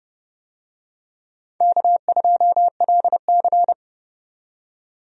Morse code, heard as a keyed tone, about 30 wpm, 700 Hz. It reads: K2LC
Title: USA Call Signs - 4 Characters - 30wpm